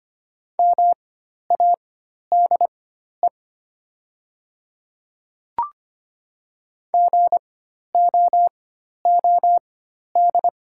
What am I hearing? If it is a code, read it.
Morse code, heard as a keyed tone, about 25 wpm, spaced as for 12 wpm, 700 Hz. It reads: MADE GOOD
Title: 4x Character Spacing - Top 200 Words - 25wpm